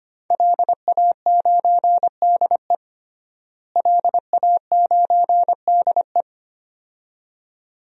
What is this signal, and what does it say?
Morse code, heard as a keyed tone, about 25 wpm, 700 Hz. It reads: LA9DE LA9DE
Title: DXCC-100 Call Signs - 5 Characters - Repeated 2x - 25wpm